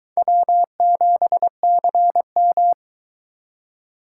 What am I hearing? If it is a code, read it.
Morse code, heard as a keyed tone, about 23 wpm, 700 Hz. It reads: W7CM